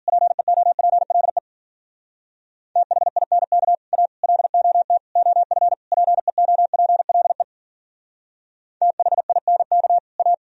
Morse code, heard as a keyed tone, about 52 wpm, 700 Hz. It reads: PEOPLE THINK A LOT OF PEOPLE THINK A